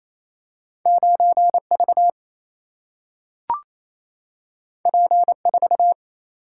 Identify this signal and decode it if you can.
Morse code, heard as a keyed tone, about 28 wpm, 700 Hz. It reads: 9 V P 4
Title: Single-Letter Single-Number 28wpm